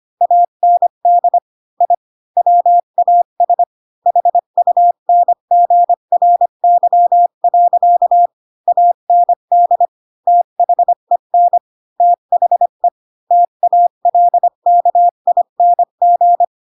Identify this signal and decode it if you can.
Morse code, heard as a keyed tone, about 25 wpm, 700 Hz. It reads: AND I WAS HUNGRY. AND THEN THE TALKING
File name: Pinocchio / Pinocchio ch7 20 25 ik1hge